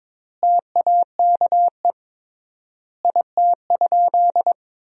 Morse code, heard as a keyed tone, about 22 wpm, 700 Hz. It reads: TAKE IT?
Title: 3x Word-Spacing - Sentences from Top 300 Words - 22wpm